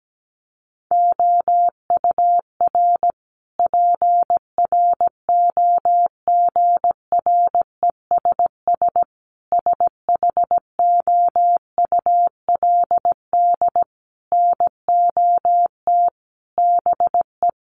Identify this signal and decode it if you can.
Morse code, heard as a keyed tone, about 17 wpm, 700 Hz. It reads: OUR PROGRESS SHOULD NOT BE
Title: Sentences from Top 1400 Words 17wpm